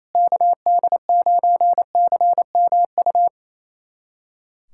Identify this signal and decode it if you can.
Morse code, heard as a keyed tone, about 28 wpm, 700 Hz. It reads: KD9CMU
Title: USA Callsign DE Callsign - 6 Characters - 28wpm